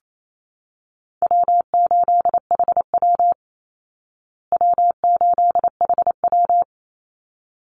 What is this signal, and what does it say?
Morse code, heard as a keyed tone, about 28 wpm, 700 Hz. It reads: W8HW W8HW
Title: Easy Call Signs Repeated 2x - 28wpm